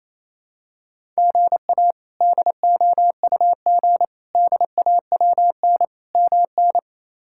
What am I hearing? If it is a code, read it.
Morse code, heard as a keyed tone, about 28 wpm, 700 Hz. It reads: GA DOUG DAWN MN
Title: K1USN SST QSO Practice - 28wpm